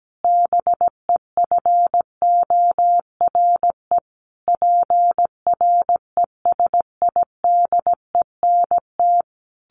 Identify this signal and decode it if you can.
Morse code, heard as a keyed tone, about 17 wpm, 700 Hz. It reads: BEFORE PRESIDENT
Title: A Sound Of Thunder 17wpm